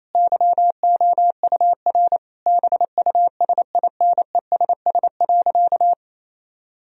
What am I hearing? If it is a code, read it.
Morse code, heard as a keyed tone, about 28 wpm, 700 Hz. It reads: YOUR BUSINESS.
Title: Sentences from Top 1000 Words - Review Part 4 - 28wpm